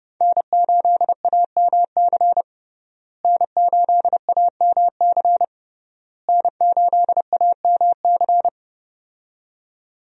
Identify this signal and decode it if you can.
Morse code, heard as a keyed tone, about 30 wpm, 700 Hz. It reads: N8AMC N8AMC N8AMC
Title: USA Call Signs - 5 Characters - Repeated 3x - 30wpm